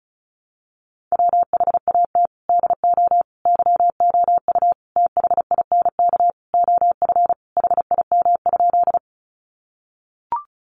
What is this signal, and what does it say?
Morse code, heard as a keyed tone, about 34 wpm, 700 Hz. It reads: WHAT DO YOU THINK OF HIM?
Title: Sentences from Top 200 Words 35wpm